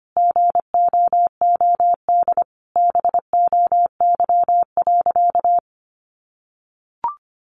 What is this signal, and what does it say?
Morse code, heard as a keyed tone, about 25 wpm, 700 Hz. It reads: GOOD BOY.
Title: Sentences from Top 700 Words 25wpm